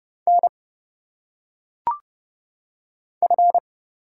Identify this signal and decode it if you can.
Morse code, heard as a keyed tone, about 30 wpm, 700 Hz. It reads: N F
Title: Single Letters - Phonetic Pronunciation 30wpm